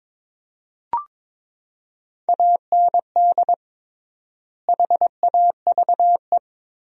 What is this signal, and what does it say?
Morse code, heard as a keyed tone, about 22 wpm, 700 Hz. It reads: AND HAVE
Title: 3x Word-Spacing - Sentences from Top 200 Words - 22wpm